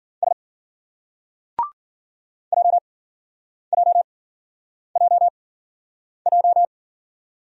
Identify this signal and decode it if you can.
Morse code, heard as a keyed tone, about 49 wpm, spaced as for 9 wpm, 700 Hz. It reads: S JJJJ
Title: Speed-Racing Single Letter-Number 40wpm